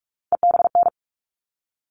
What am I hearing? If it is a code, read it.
Morse code, heard as a keyed tone, about 44 wpm, 700 Hz. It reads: EBN
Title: All Cut Numbers - 3-Digit 45wpm